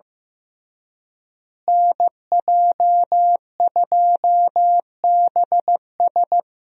Morse code, heard as a keyed tone, about 15 wpm, 700 Hz. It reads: NJ2BS
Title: USA Call Signs - 5 Characters - Repeated 3x - 15wpm